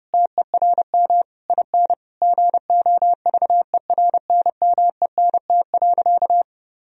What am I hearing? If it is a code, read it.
Morse code, heard as a keyed tone, about 32 wpm, 700 Hz. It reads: TERM IN GOVERNMENT.
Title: Sentences from Top 1000 Words - Review Part 4 - 30wpm